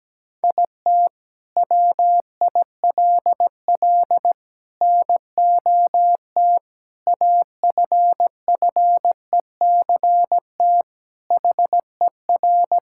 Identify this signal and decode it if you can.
Morse code, heard as a keyed tone, about 17 wpm, 700 Hz. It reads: IT WILL NOT AFFECT HER
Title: Sentences from Top 1000 Words - Review Part 6 - 17wpm